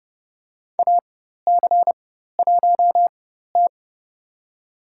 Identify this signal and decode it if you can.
Morse code, heard as a keyed tone, about 32 wpm, spaced as for 14 wpm, 700 Hz. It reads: AC1T
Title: USA Call Signs - 4 Characters - 4x Character Spacing - 30wpm